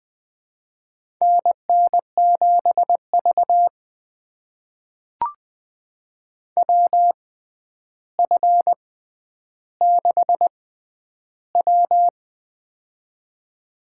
Morse code, heard as a keyed tone, about 21 wpm, spaced as for 10 wpm, 700 Hz. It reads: NN7V WF6W
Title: USA Call Signs - 4 Characters - 6x Character Spacing - 20wpm